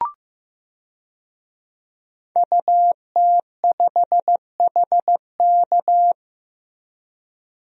Morse code, heard as a keyed tone, about 15 wpm, 700 Hz. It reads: UT5HK
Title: DXCC-100 Call Signs - 5 Characters - Repeated 2x - 15wpm